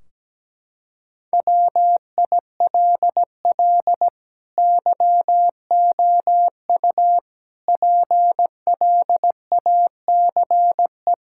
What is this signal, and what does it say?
Morse code, heard as a keyed tone, about 17 wpm, 700 Hz. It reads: WILL YOU PLACE
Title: Sentences from Top 1600 Words 17wpm